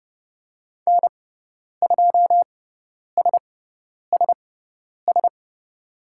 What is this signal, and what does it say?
Morse code, heard as a keyed tone, about 30 wpm, spaced as for 10 wpm, 700 Hz. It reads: N2SSS
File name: Lesson 17g - Letter u - Call Signs 30wpm